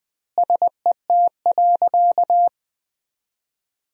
Morse code, heard as a keyed tone, about 20 wpm, 700 Hz. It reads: SET.
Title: 3x Word-Spacing - Sentences from Top 400 Words - 20wpm